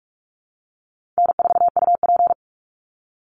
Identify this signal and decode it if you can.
Morse code, heard as a keyed tone, about 44 wpm, 700 Hz. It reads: N4UP
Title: Call Signs - 45wpm